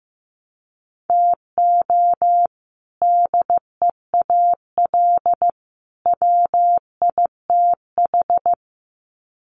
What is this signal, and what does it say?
Morse code, heard as a keyed tone, about 14 wpm, 700 Hz. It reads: TO DEAL WITH
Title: Sets of 3 Words 15wpm